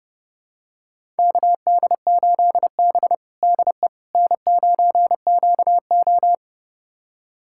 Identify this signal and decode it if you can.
Morse code, heard as a keyed tone, about 32 wpm, 700 Hz. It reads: KD8B DE N9QO
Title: USA Callsign DE Callsign - 4 Characters - 30wpm